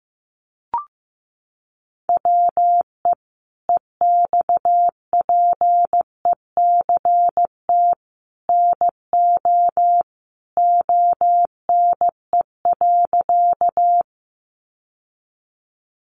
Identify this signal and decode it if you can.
Morse code, heard as a keyed tone, about 15 wpm, 700 Hz. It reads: WE EXPECT NO ONE.
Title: Sentences from Top 1100 Words 15wpm